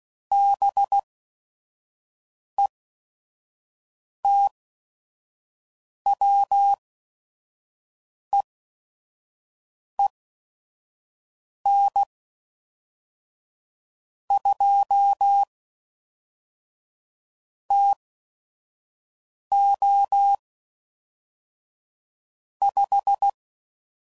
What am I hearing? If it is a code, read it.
Morse code, heard as a keyed tone, about 16 wpm, spaced as for 5 wpm, 800 Hz. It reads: BETWEEN 2 TO 5
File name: CW Code Groups @5WPM / Weather Sets / Weather 5 12